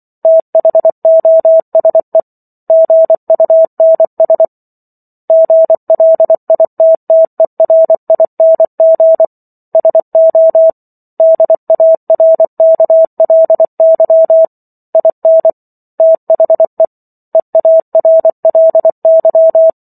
Morse code, heard as a keyed tone, about 24 wpm, 650 Hz. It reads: THOSE GUNS GLITTERING SO DARKLY IN THE EARLY
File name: Chapter 15 - 24 WPM